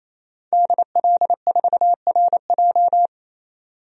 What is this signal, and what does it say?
Morse code, heard as a keyed tone, about 28 wpm, 700 Hz. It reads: DL4RJ
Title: DXCC-100 Call Signs - 5 Characters - 28wpm